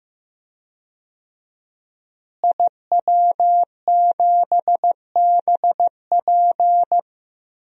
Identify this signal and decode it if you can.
Morse code, heard as a keyed tone, about 15 wpm, 700 Hz. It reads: IW7BP
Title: DXCC-100 Call Signs - 5 Characters - Repeated 3x - 15wpm